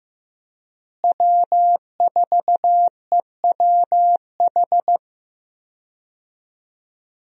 Morse code, heard as a keyed tone, about 15 wpm, 700 Hz. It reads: W4EWH
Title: USA Call Signs - 5 Characters - Repeated 3x - 15wpm